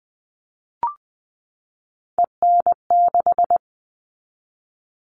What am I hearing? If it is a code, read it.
Morse code, heard as a keyed tone, about 20 wpm, 700 Hz. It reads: EN6
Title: All Cut Numbers - 3-Digit 20wpm